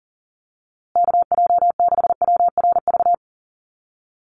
Morse code, heard as a keyed tone, about 40 wpm, 700 Hz. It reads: KJ6WRV
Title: USA Call Signs Repeated 2x 40wpm